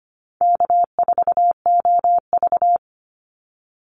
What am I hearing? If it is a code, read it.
Morse code, heard as a keyed tone, about 25 wpm, 700 Hz. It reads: K4OV
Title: Easy Call Signs - 25wpm